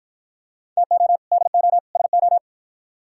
Morse code, heard as a keyed tone, about 51 wpm, 700 Hz. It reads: TO DO SO